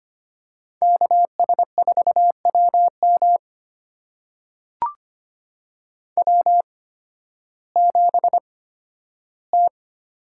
Morse code, heard as a keyed tone, about 25 wpm, spaced as for 13 wpm, 700 Hz. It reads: KS4WM W7T
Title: USA Call Signs - 5 Characters - 8x Character Spacing - 25wpm